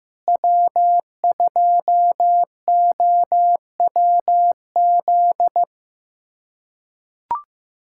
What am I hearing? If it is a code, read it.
Morse code, heard as a keyed tone, about 16 wpm, 700 Hz. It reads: W2OWZ
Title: USA Callsign DE Callsign 15wpm